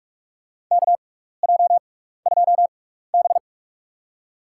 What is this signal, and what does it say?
Morse code, heard as a keyed tone, about 46 wpm, spaced as for 15 wpm, 700 Hz. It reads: KJ2B
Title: USA Call Signs - 4 Characters - 6x Character Spacing - 45wpm